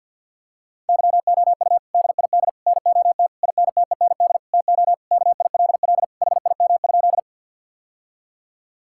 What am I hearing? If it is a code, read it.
Morse code, heard as a keyed tone, about 50 wpm, 700 Hz. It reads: YOU DID NOT INTEND TO KILL HIM?
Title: Sentences from Top 1600 Words 50wpm